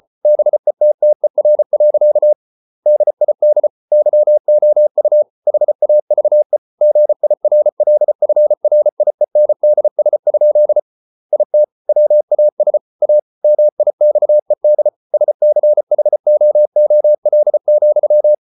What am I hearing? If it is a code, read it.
Morse code, heard as a keyed tone, about 34 wpm, 600 Hz. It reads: BETTER. DID YOU HAVE GIRLFRIENDS? IT WAS A MIXED SCHOOL,